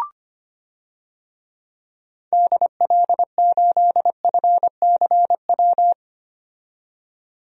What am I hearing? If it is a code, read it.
Morse code, heard as a keyed tone, about 25 wpm, 700 Hz. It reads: DL8FCW
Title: DXCC-100 Call Signs - 6 Characters - 25wpm